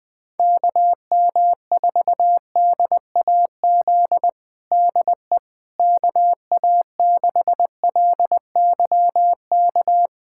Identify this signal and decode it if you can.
Morse code, heard as a keyed tone, about 20 wpm, 700 Hz. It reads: KM4DAZ DE KA6LYK
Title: USA Callsign DE Callsign 20wpm